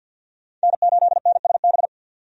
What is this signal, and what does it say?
Morse code, heard as a keyed tone, about 46 wpm, 700 Hz. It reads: N8NSB
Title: USA Call Signs - 5 Characters - 50wpm